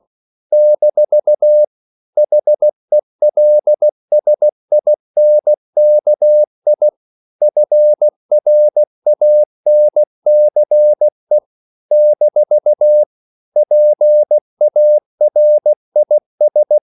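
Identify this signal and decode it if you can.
Morse code, heard as a keyed tone, about 16 wpm, 600 Hz. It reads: - HELSINKI FRANCE - PARIS
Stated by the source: CW practice of national capitals at 16 words per minute, from ON6ZQ.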